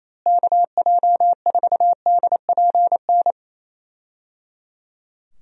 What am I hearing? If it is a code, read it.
Morse code, heard as a keyed tone, about 28 wpm, 700 Hz. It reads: KJ4DPN
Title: USA Call Signs - 6 Characters - Repeated 3x - 28wpm